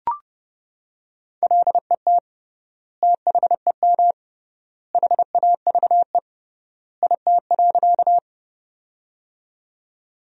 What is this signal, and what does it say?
Morse code, heard as a keyed tone, about 30 wpm, 700 Hz. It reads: LET THEM HAVE IT.
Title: 3x Word-Spacing - Sentences from Top 300 Words - 30wpm